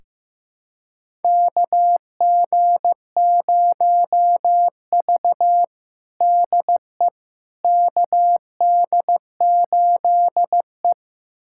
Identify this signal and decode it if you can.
Morse code, heard as a keyed tone, about 16 wpm, 700 Hz. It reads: KG0V DE KD8E
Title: USA Callsign DE Callsign - 4 Characters - 15wpm